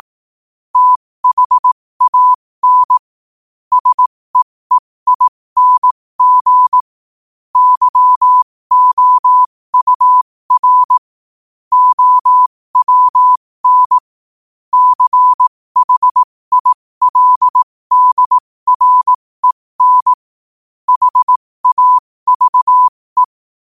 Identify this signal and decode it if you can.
Morse code, heard as a keyed tone, about 18 wpm, spaced as for 15 wpm, 1000 Hz. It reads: THAN SEEING YOUR OWN CHILDREN HAVE
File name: qotdfiles / 15wpm